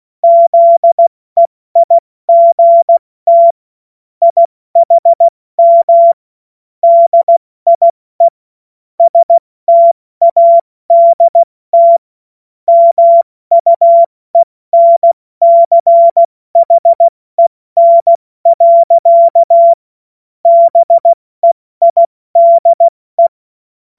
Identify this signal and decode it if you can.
Morse code, heard as a keyed tone, about 16 wpm, 680 Hz. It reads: ZEIGT IHM DIE STADT MUENCHEN. BEIDE
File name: Morsekurs / Witze / Witz 30. Dezember 2025 70 F80